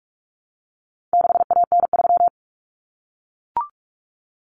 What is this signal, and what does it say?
Morse code, heard as a keyed tone, about 44 wpm, 700 Hz. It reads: 6AN3
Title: Common Cut Numbers - 4-Digit 45wpm